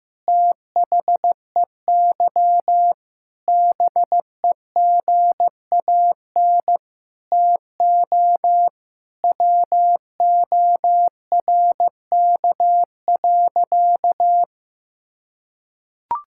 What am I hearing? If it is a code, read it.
Morse code, heard as a keyed tone, about 15 wpm, 700 Hz. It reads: THEY BEGAN TO WORK.
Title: Sentences from Top 1000 Words - Review Part 6 - 15wpm